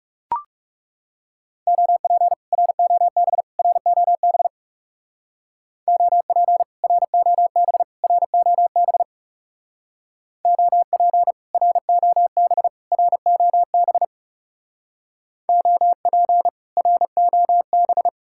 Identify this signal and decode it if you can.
Morse code, heard as a keyed tone, about 38 wpm, 700 Hz. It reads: OP ROB ROB OP ROB ROB OP ROB ROB OP ROB